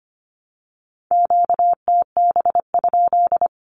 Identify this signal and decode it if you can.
Morse code, heard as a keyed tone, about 25 wpm, 700 Hz. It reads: QTB?
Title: Q-Codes Repeated 3x 25wpm